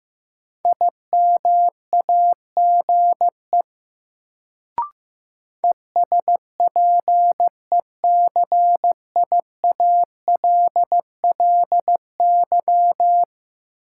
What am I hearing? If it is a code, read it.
Morse code, heard as a keyed tone, about 15 wpm, 700 Hz. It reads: IMAGE ESPECIALLY
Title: Sentences from Top 700 Words - New Word Drill - 15wpm